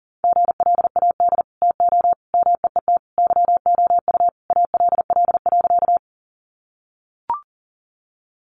Morse code, heard as a keyed tone, about 40 wpm, 700 Hz. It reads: GLAD TO MEET YOU ALL.
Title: Sentences from Top 1100 Words 40wpm